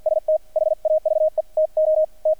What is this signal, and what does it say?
Morse code, heard as a keyed tone, about 50 wpm, spaced as for 44 wpm, 620 Hz. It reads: ST HAVE TO T
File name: archived / CRSnet